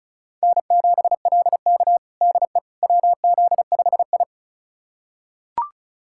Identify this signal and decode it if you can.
Morse code, heard as a keyed tone, about 34 wpm, 700 Hz. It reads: N7LK DE WZ5I
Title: USA Callsign DE Callsign - 4 Characters - 35wpm